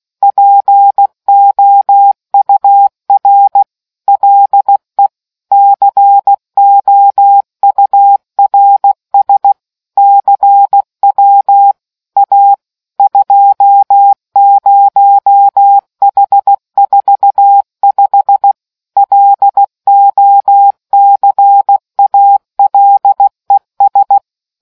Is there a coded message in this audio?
Yes: Morse code from 0.2 to 24.2 s, heard as a keyed tone, about 16 wpm, 800 Hz. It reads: POUR LE COURS CW A 20H45 LOCALES